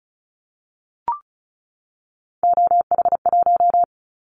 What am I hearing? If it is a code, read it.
Morse code, heard as a keyed tone, about 36 wpm, 700 Hz. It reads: OH1
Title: Callsign Prefixes for Top 50 DXCC - 35wpm